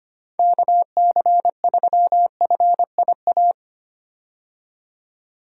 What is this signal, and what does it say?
Morse code, heard as a keyed tone, about 25 wpm, 700 Hz. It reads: KC3FIA
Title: USA Call Signs - 6 Characters - 25wpm